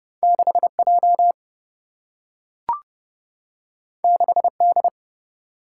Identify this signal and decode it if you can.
Morse code, heard as a keyed tone, about 30 wpm, 700 Hz. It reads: 6 J 6 D
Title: Single-Letter Single-Number 30wpm